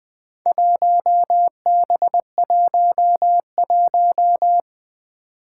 Morse code, heard as a keyed tone, about 21 wpm, 700 Hz. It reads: 1B11
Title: Four Challenging Letter-Number Combinations 20wpm